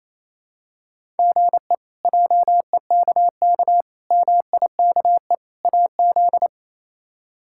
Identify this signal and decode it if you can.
Morse code, heard as a keyed tone, about 28 wpm, 700 Hz. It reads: GE JEKK MIKE AZ
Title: K1USN SST QSO Practice - 28wpm